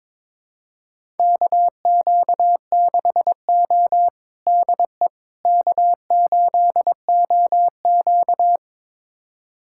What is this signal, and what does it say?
Morse code, heard as a keyed tone, about 22 wpm, 700 Hz. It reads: KQ6O DE K8OQ
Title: USA Callsign DE Callsign - 4 Characters - 22wpm